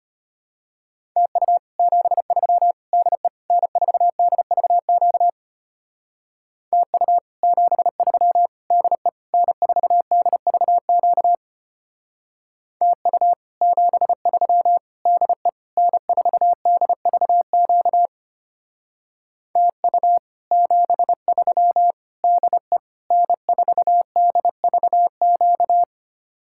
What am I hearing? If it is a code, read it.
Morse code, heard as a keyed tone, about 33 wpm, 700 Hz. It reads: TU 73 DE N4DVQ TU 73 DE N4DVQ TU 73 DE N4DVQ TU 73 DE N4DVQ